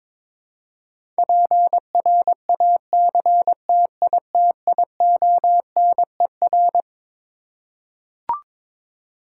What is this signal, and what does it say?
Morse code, heard as a keyed tone, about 22 wpm, 700 Hz. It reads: PRACTITIONER
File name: Twelve Letter Words - 22wpm